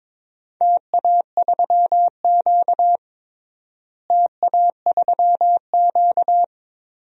Morse code, heard as a keyed tone, about 21 wpm, 700 Hz. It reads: TA3Q TA3Q
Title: DXCC-100 Call Signs - 4 Characters - Repeated 3x - 22wpm